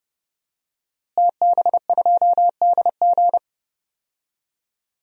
Morse code, heard as a keyed tone, about 30 wpm, 700 Hz. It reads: TB2DG
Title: DXCC-100 Call Signs - 5 Characters - Repeated 2x - 30wpm